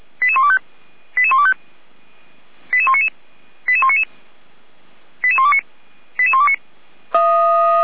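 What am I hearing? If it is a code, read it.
Five-tone selective call (ZVEI), heard as quick runs of five tones, tones of about 70 ms. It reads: 88125 88125 88199 88199 88129 88129
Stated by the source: ZVEI five-tone call 88211 received on SDR.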